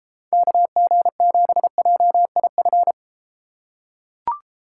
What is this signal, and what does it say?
Morse code, heard as a keyed tone, about 33 wpm, 700 Hz. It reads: KG7JIF